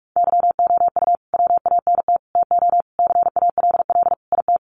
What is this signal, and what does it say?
Morse code, heard as a keyed tone, about 45 wpm, 700 Hz. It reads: YOU WANT TO CALL IT
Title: Sets of 5 Words Encore2 45wpm